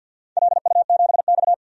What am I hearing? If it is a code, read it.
Morse code, heard as a keyed tone, about 46 wpm, 700 Hz. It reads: LU7X
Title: DXCC-100 Call Signs - 4 Characters - Repeated 3x - 50wpm